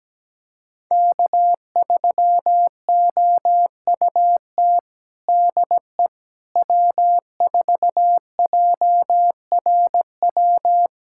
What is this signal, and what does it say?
Morse code, heard as a keyed tone, about 17 wpm, 700 Hz. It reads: K3OUT DE W4JRW
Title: USA Callsign DE Callsign - 5 Characters - 17wpm